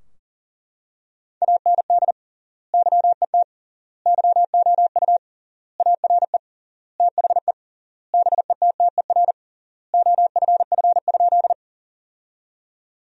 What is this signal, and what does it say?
Morse code, heard as a keyed tone, about 40 wpm, 700 Hz. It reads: AND YET YOU ARE THE BETTER OFF?
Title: 3x Word-Spacing - Sentences from Top 500 Words - 40wpm